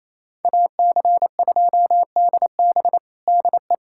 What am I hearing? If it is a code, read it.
Morse code, heard as a keyed tone, about 28 wpm, 700 Hz. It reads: AC2DB DE
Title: USA Callsign DE Callsign - 5 Characters - 28wpm